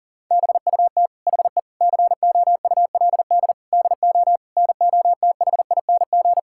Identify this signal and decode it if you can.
Morse code, heard as a keyed tone, about 37 wpm, 700 Hz. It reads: BUT HE COULD DO NOTHING
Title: Sentences from Top 1600 Words 40wpm